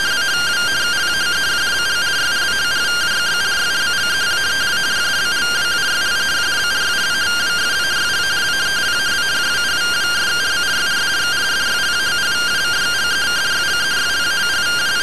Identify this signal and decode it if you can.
Two-tone teleprinter signal, about 170 Hz shift.
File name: findings / CT SAGITAL 27.gen-22 rtty rx